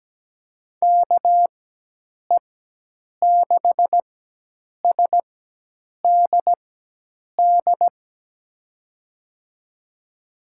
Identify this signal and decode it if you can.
Morse code, heard as a keyed tone, about 17 wpm, spaced as for 8 wpm, 700 Hz. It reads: KE6SDD
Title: USA Call Signs - 6 Characters - 4x Character Spacing - 17wpm